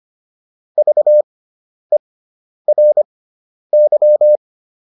Morse code, heard as a keyed tone, about 25 wpm, spaced as for 10 wpm, 600 Hz. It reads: VERY